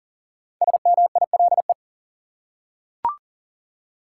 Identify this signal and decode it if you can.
Morse code, heard as a keyed tone, about 40 wpm, 700 Hz. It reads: SMILE
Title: Five Letter Words - 40wpm